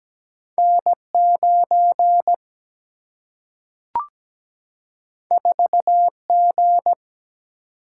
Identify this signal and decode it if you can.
Morse code, heard as a keyed tone, about 17 wpm, 700 Hz. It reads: N9 4G
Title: Single-Letter Single-Number 17wpm